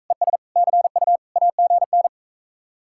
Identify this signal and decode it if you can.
Morse code, heard as a keyed tone, about 45 wpm, 700 Hz. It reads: ES CU AGN